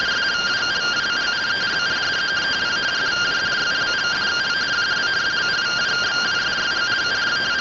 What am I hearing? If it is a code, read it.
Two-tone teleprinter signal, about 170 Hz shift.
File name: findings / AXIAL fr130 3mm 409.gen-23 rtty rx